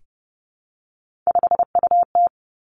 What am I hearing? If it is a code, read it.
Morse code, heard as a keyed tone, about 30 wpm, 700 Hz. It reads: 5UT
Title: Three Challenging Letter-Number Combinations 30wpm